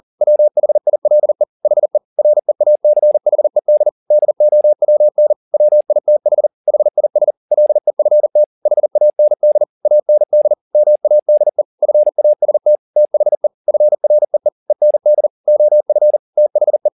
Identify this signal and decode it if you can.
Morse code, heard as a keyed tone, about 40 wpm, 600 Hz. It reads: WHILE HE REACHED DOWN WITH HIS LEFT HAND AND MADE FAST THE FREE END OF THE